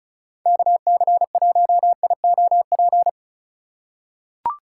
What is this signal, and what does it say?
Morse code, heard as a keyed tone, about 36 wpm, 700 Hz. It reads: KC1IOP
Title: USA Call Signs - 3x Character Spacing - 35wpm